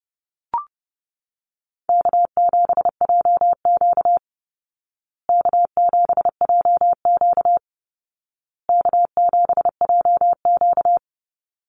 Morse code, heard as a keyed tone, about 30 wpm, 700 Hz. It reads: K7JQ K7JQ K7JQ
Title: Easy Call Signs Repeated 3x - 30wpm